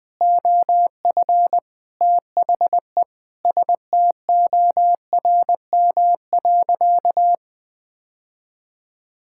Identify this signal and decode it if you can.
Morse code, heard as a keyed tone, about 20 wpm, 700 Hz. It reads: OF THE STORM.
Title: Sentences with 4-Word Phrases - 20wpm